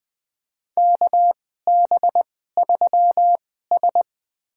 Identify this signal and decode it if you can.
Morse code, heard as a keyed tone, about 20 wpm, spaced as for 14 wpm, 700 Hz. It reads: KB3S
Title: USA Call Signs - 4 Characters - 2x Character Spacing - 20wpm